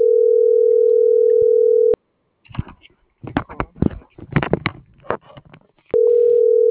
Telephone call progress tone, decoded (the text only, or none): ringback tone